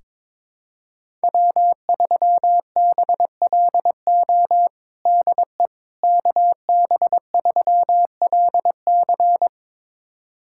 Morse code, heard as a keyed tone, about 21 wpm, 700 Hz. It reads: W3BLO DE KB3LC
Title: USA Callsign DE Callsign - 5 Characters - 22wpm